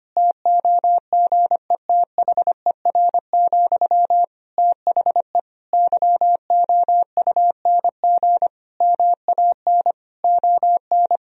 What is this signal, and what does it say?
Morse code, heard as a keyed tone, about 25 wpm, 700 Hz. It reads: TOGETHER, THE YOUNG MAN ON
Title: Sentences from Top 1000 Words - Review Part 2 - 25wpm